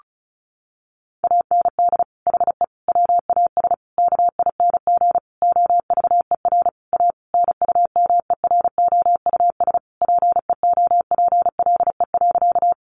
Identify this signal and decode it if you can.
Morse code, heard as a keyed tone, about 34 wpm, 700 Hz. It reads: AND HE WAS KING OVER A NUMEROUS PEOPLE.
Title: Sentences from Top 1400 Words 35wpm